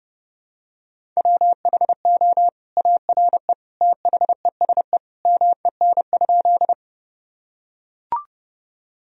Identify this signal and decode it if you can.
Morse code, heard as a keyed tone, about 30 wpm, 700 Hz. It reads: WHO ARE THESE MEN?
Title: Sentences from Top 1000 Words - Review Part 1 - 30wpm